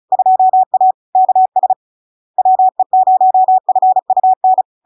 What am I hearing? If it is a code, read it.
Morse code, heard as a keyed tone, about 36 wpm, 750 Hz. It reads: 2A KS WE0FUN
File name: LARK-FD-2022-Log-in-CW-35wpm-750-Hz